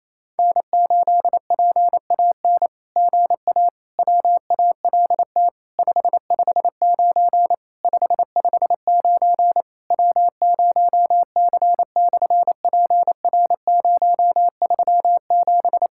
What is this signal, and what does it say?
Morse code, heard as a keyed tone, about 28 wpm, 700 Hz. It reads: N8PAN GA WALT 559 559 W0C/PR037
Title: SOTA QSO Practice 28wpm